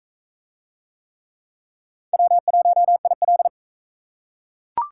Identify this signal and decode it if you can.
Morse code, heard as a keyed tone, about 42 wpm, 700 Hz. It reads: W1IL